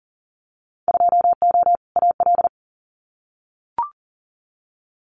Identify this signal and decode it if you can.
Morse code, heard as a keyed tone, about 41 wpm, 700 Hz. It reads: 2O AL
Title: Winter Field Day Exchanges 40wpm